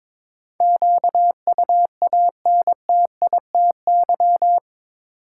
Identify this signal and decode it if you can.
Morse code, heard as a keyed tone, about 22 wpm, 700 Hz. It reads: QUANTITY
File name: practice / 22wpm / Eight Letter Words - 22wpm